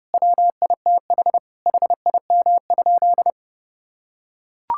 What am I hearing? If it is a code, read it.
Morse code, heard as a keyed tone, about 30 wpm, 700 Hz. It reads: WITH HIM?
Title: Sentences from Top 1000 Words - Review Part 6 - 30wpm